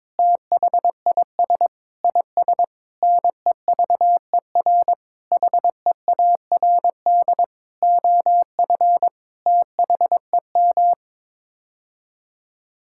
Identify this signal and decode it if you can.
Morse code, heard as a keyed tone, about 22 wpm, 700 Hz. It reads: THIS IS NEVER HEARD OF THEM
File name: Sets of 9 Words - 22wpm